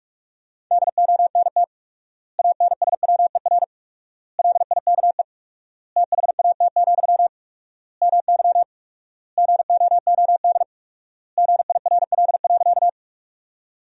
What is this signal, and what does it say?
Morse code, heard as a keyed tone, about 43 wpm, 700 Hz. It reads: DONT ANSWER LIKE THAT, MY GOOD GIRL.
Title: 4x Word-Spacing - Sentences from Top 400 Words - 45wpm